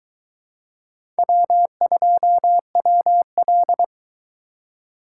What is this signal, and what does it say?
Morse code, heard as a keyed tone, about 23 wpm, 700 Hz. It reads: W2WL